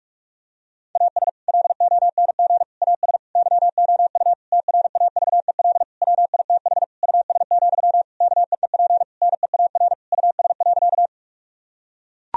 Morse code, heard as a keyed tone, about 42 wpm, 700 Hz. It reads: AS LONG AS YOU TRAVEL WITH US, KEEP NEAR US.
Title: Sentences from Top 1000 Words - Review Part 2 - 45wpm